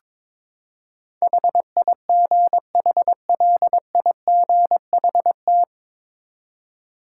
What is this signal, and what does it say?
Morse code, heard as a keyed tone, about 22 wpm, 700 Hz. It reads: HIGHLIGHT
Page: Morse Code Ninja practice 22wpm